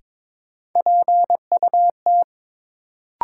Morse code, heard as a keyed tone, about 22 wpm, 700 Hz. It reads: PUT
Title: Sentences from Top 200 Words - New Word Drill - 22wpm